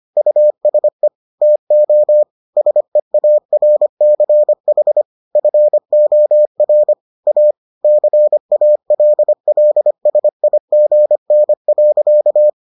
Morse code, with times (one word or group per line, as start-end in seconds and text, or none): 0.2-1.1 USE
1.4-2.2 TO
2.6-5.0 SEARCH
5.4-6.9 FOR
7.3-7.5 A
7.8-12.5 CALLSIGN.